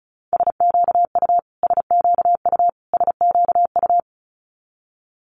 Morse code, heard as a keyed tone, about 34 wpm, 700 Hz. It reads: SQU SQU SQU
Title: Multi-letter Phonemes Repeated 3x 35wpm